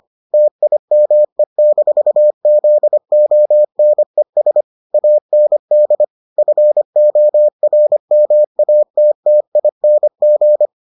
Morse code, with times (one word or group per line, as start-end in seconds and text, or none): 0.3-4.6 TIME-ZONES
4.9-6.0 AND
6.4-10.7 FORMATTING